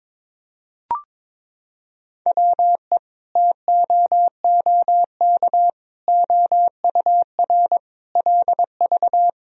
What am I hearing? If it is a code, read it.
Morse code, heard as a keyed tone, about 22 wpm, 700 Hz. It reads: WE TOOK OUR LV
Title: Sentences from Top 400 Words and Abbreviations 22wpm